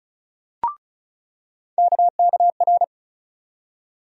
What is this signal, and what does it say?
Morse code, heard as a keyed tone, about 36 wpm, 700 Hz. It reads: KKR
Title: USA Call Sign Postfixes - 3 Letters 35wpm